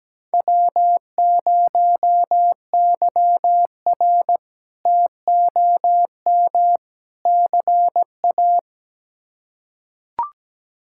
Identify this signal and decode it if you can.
Morse code, heard as a keyed tone, about 17 wpm, 700 Hz. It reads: W0YR TOM CA
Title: K1USN SST QSO Practice - 17wpm